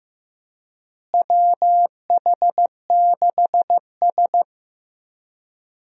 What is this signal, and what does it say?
Morse code, heard as a keyed tone, about 15 wpm, 700 Hz. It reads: WH6S
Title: USA Call Signs - 4 Characters - Repeated 2x - 15wpm